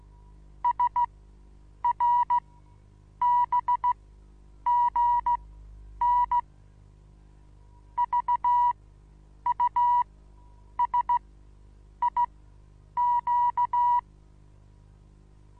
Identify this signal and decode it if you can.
Morse code, heard as a keyed tone, about 16 wpm, spaced as for 8 wpm, 970 Hz. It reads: SRBGN VUSIQ
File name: Onlinelehrgang / morsen / DJ4UF Morsen25